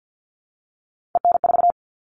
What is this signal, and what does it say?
Morse code, heard as a keyed tone, about 50 wpm, 700 Hz. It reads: EN4
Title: All Cut Numbers - 3-Digit 50wpm